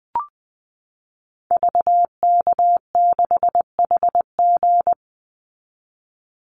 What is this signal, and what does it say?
Morse code, heard as a keyed tone, about 19 wpm, 700 Hz. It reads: VK6HG
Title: Call Signs Repeated 3x - 20wpm